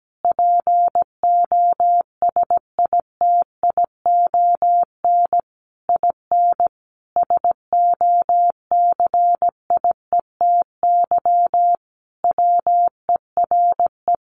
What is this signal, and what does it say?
Morse code, heard as a keyed tone, about 17 wpm, 700 Hz. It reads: POSITION IN SOCIETY WERE
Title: Sentences from Top 900 Words 17wpm